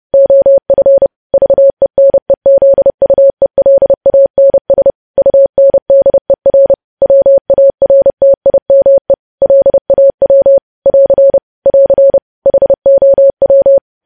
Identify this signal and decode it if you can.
Morse code, heard as a keyed tone, about 30 wpm, 550 Hz. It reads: OF VENEZUELANS UNDER WARTIME LAW <AR> <AR> HOW
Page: news headlines in morse code at 30 wpm